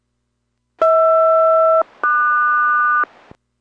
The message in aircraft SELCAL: HR-QS